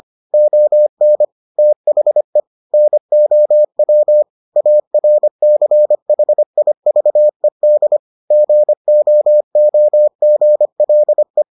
Morse code, heard as a keyed tone, about 25 wpm, 600 Hz. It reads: ON THE NOW ARCHIVED GOOGLE